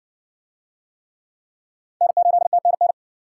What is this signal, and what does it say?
Morse code, heard as a keyed tone, about 55 wpm, 700 Hz. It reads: N8TNN